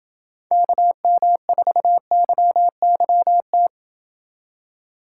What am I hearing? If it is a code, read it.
Morse code, heard as a keyed tone, about 29 wpm, 700 Hz. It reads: KM4YYT